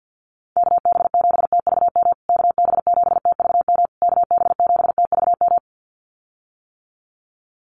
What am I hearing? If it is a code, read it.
Morse code, heard as a keyed tone, about 50 wpm, 700 Hz. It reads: KB7TVM KB7TVM KB7TVM
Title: USA Call Signs Repeated 3x 50wpm